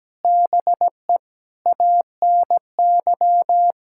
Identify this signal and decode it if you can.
Morse code, heard as a keyed tone, about 17 wpm, 700 Hz. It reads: BE ANY
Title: Sentences from Top 1600 Words 17wpm